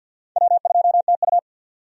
Morse code, heard as a keyed tone, about 48 wpm, 700 Hz. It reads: W2TU
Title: USA Call Signs - 4 Characters - 50wpm